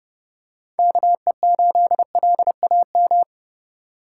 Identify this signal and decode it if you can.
Morse code, heard as a keyed tone, about 30 wpm, 700 Hz. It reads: KE8LAM
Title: USA Call Signs - 6 Characters - 2x Character Spacing - 30wpm